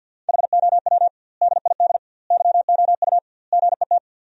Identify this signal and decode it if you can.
Morse code, heard as a keyed tone, about 46 wpm, 700 Hz. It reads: HOW DID YOU GET
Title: 2x Word-Spacing - Sentences from Top 200 Words - 50wpm